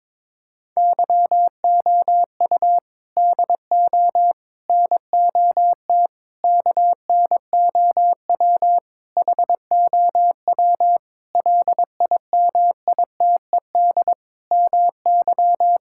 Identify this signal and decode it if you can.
Morse code, heard as a keyed tone, about 21 wpm, 700 Hz. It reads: YOU DO NOT KNOW HOW LIMITED MY
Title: Sentences from Top 1000 Words - Review Part 6 - 22wpm